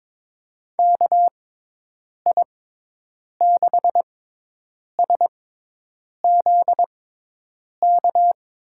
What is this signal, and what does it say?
Morse code, heard as a keyed tone, about 22 wpm, spaced as for 8 wpm, 700 Hz. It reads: KI6SZK
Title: USA Call Signs - 6 Characters - 6x Character Spacing - 22wpm